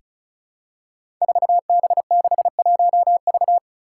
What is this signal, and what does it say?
Morse code, heard as a keyed tone, about 36 wpm, 700 Hz. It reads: 4B61V
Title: Five Challenging Letter-Number Combinations 35wpm